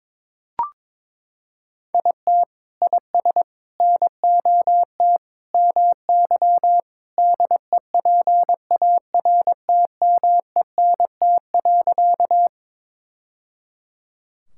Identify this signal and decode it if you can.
Morse code, heard as a keyed tone, about 22 wpm, 700 Hz. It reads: IT IS NOT MY DEPARTMENT.
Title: Sentences from Top 1000 Words - Review Part 5 - 22wpm